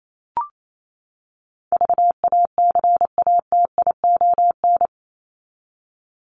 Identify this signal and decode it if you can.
Morse code, heard as a keyed tone, about 28 wpm, 700 Hz. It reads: VACATION
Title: Top 1501-1750 Words 28wpm